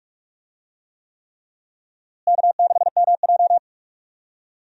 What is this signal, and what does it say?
Morse code, heard as a keyed tone, about 46 wpm, 700 Hz. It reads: K6MJ
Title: USA Call Signs - 4 Characters - Repeated 3x - 45wpm